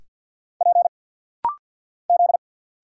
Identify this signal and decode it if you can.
Morse code, heard as a keyed tone, about 50 wpm, 700 Hz. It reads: P Z
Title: Single Letters - Rapid-Fire 50wpm